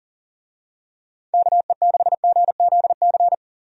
Morse code, heard as a keyed tone, about 40 wpm, 700 Hz. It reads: KE6GZC
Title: USA Call Signs - 6 Characters - 40wpm